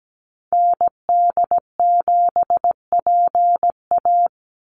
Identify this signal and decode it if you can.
Morse code, heard as a keyed tone, about 17 wpm, 700 Hz. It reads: ND7PA
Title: NA SOTA Activator Callsigns 17wpm